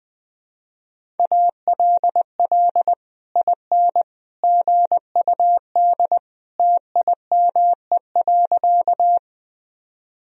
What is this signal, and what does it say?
Morse code, heard as a keyed tone, about 20 wpm, 700 Hz. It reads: ALL IN GUD TIME.
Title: Sentences from Top 200 Words and Abbreviations 20wpm